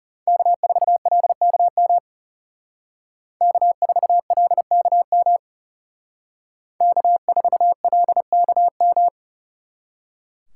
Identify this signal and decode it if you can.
Morse code, heard as a keyed tone, about 35 wpm, 700 Hz. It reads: K4LKM K4LKM K4LKM
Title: Speed Racing - USA 1x3 Call Signs 30wpm